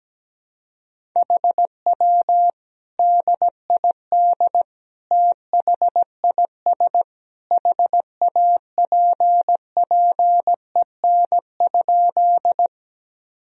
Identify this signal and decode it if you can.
Morse code, heard as a keyed tone, about 17 wpm, 700 Hz. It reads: HW DID THIS HAPPEN?
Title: Sentences from Top 400 Words and Abbreviations 17wpm